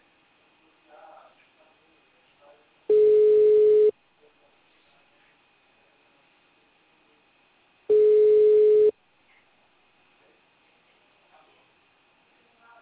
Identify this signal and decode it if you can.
Ringback tone (Europe, 425 Hz), with speech.